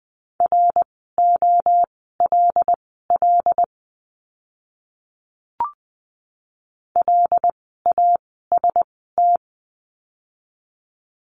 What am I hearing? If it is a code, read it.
Morse code, heard as a keyed tone, about 20 wpm, spaced as for 14 wpm, 700 Hz. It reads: ROLL LAST
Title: 075 Lesson 13f - Letter L - Words 20wpm